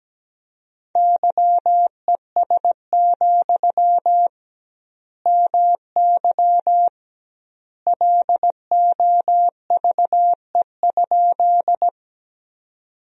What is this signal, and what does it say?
Morse code, heard as a keyed tone, about 17 wpm, 700 Hz. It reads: YES, MY LOVE?
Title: 2x Word-Spacing - Sentences from Top 500 Words - 17wpm